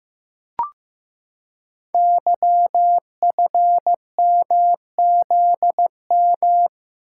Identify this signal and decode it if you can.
Morse code, heard as a keyed tone, about 15 wpm, 700 Hz. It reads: YFMZM
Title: Code Groups - 5 Letters - 15wpm